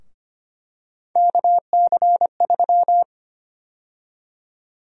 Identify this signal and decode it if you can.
Morse code, heard as a keyed tone, about 25 wpm, 700 Hz. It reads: KC3
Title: USA Call Sign Prefixes - Repeated 2x - 25wpm